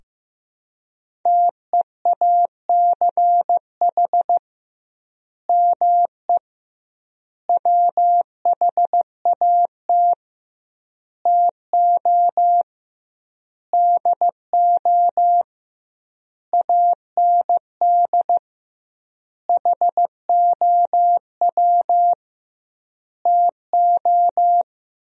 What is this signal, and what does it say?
Morse code, heard as a keyed tone, about 15 wpm, 700 Hz. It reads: TEACH ME WHAT TO DO AND HOW TO
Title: 2x Word-Spacing - Sentences from Top 500 Words - 15wpm